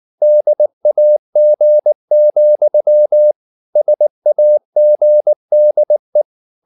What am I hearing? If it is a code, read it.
Morse code, heard as a keyed tone, about 19 wpm, 600 Hz. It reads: DAG, SAGDE
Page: Ferie 19 wpm